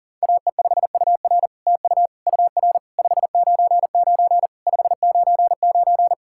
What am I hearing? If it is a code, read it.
Morse code, heard as a keyed tone, about 39 wpm, 700 Hz. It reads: AE5UR TU UR 599 599
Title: SOTA QSO Practice 40wpm